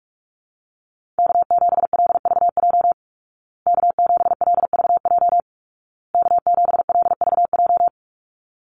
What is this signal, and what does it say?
Morse code, heard as a keyed tone, about 44 wpm, 700 Hz. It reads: K7LVJ K7LVJ K7LVJ
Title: Call Signs Repeated 2x - 45wpm